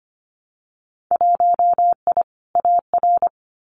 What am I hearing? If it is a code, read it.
Morse code, heard as a keyed tone, about 25 wpm, 700 Hz. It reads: 1I AR
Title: Winter Field Day Exchanges Encore 25wpm